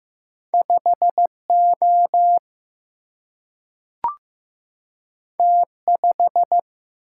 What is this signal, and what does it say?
Morse code, heard as a keyed tone, about 15 wpm, 700 Hz. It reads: 5 O T 5
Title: Single-Letter Single-Number 15wpm